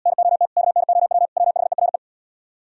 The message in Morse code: NOT KNOW WHERE

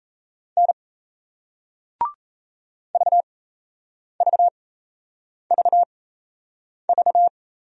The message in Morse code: N VVVV

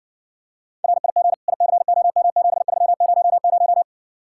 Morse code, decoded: RIG ICOM 7300

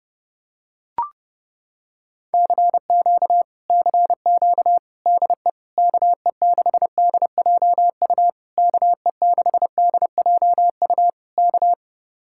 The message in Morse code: CQ CQ DE KE6DJU KE6DJU K

